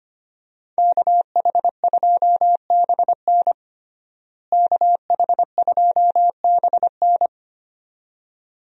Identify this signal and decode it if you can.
Morse code, heard as a keyed tone, about 25 wpm, 700 Hz. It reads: KH2BN KH2BN
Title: USA Call Signs - 5 Characters - Repeated 2x - 25wpm